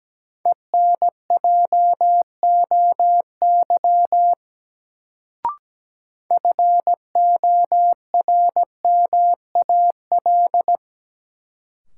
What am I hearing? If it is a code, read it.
Morse code, heard as a keyed tone, about 17 wpm, 700 Hz. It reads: ENJOY FORMAL
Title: Sentences from Top 1300 Words - New Word Drill Alt - 17wpm